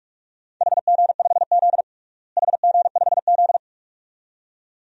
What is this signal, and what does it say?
Morse code, heard as a keyed tone, about 41 wpm, 700 Hz. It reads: HG5Z HG5Z
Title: DXCC-100 Call Signs - 4 Characters - Repeated 2x - 45wpm